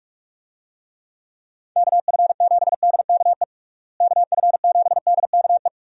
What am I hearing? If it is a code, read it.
Morse code, heard as a keyed tone, about 46 wpm, 700 Hz. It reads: KF7DKE KF7DKE
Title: USA Call Signs - 6 Characters - Repeated 3x - 45wpm